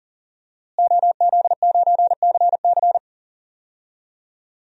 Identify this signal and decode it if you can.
Morse code, heard as a keyed tone, about 40 wpm, 700 Hz. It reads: OZ9CC
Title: DXCC-100 Call Signs - 5 Characters - Repeated 2x - 40wpm